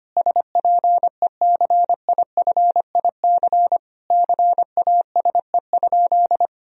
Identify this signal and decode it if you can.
Morse code, heard as a keyed tone, about 25 wpm, 700 Hz. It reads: SPECIFIC CASE?
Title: Sentences from Top 1000 Words - Review Part 3 - 25wpm